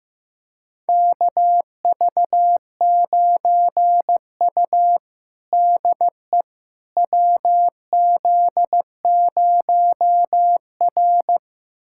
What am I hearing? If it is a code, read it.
Morse code, heard as a keyed tone, about 15 wpm, 700 Hz. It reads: KV9U DE WZ0R
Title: USA Callsign DE Callsign - 4 Characters - 15wpm